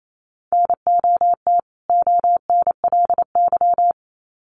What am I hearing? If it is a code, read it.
Morse code, heard as a keyed tone, about 28 wpm, 700 Hz. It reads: NOT ONLY
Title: Sets of 2 Words 28wpm